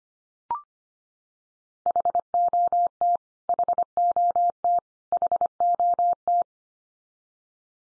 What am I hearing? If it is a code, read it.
Morse code, heard as a keyed tone, about 25 wpm, 700 Hz. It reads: HOT HOT HOT
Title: Top 100 Words 25wpm x3